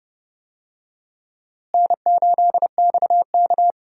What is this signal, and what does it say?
Morse code, heard as a keyed tone, about 30 wpm, 700 Hz. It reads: N8XK
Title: USA Call Signs - 4 Characters - Repeated 2x - 30wpm